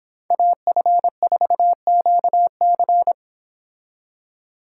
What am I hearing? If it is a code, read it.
Morse code, heard as a keyed tone, about 26 wpm, 700 Hz. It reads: AF4QC